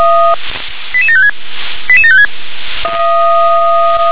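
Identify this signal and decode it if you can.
Five-tone selective call (ZVEI), heard as quick runs of five tones, tones of about 70 ms. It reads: 88756 88756
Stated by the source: ZVEI five-tone call received on an SDR.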